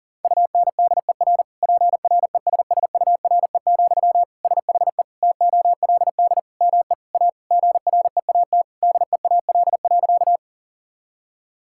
Morse code, heard as a keyed tone, about 40 wpm, 700 Hz. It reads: UNDER PRESSURE, SHE TOLD ME A GREAT DEAL.
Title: Sentences from Top 1000 Words - Review Part 6 - 40wpm